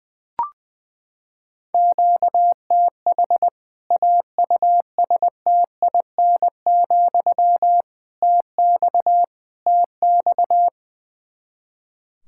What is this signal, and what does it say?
Morse code, heard as a keyed tone, about 21 wpm, 700 Hz. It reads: QTH AUSTIN, TX TX
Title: K1USN SST QSO Practice 20wpm